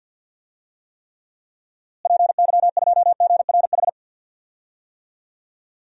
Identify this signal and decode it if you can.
Morse code, heard as a keyed tone, about 46 wpm, 700 Hz. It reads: PY2GRH
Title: DXCC-100 Call Signs - 6 Characters - Repeated 2x - 50wpm